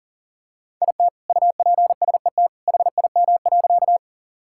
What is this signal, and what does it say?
Morse code, heard as a keyed tone, about 40 wpm, 700 Hz. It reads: IT UPSET HIM.
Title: Sentences from Top 1000 Words - Review Part 5 - 40wpm